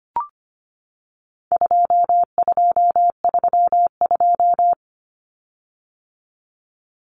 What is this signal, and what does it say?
Morse code, heard as a keyed tone, about 25 wpm, 700 Hz. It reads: 2232
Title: Common Cut Numbers - 4-Digit 25wpm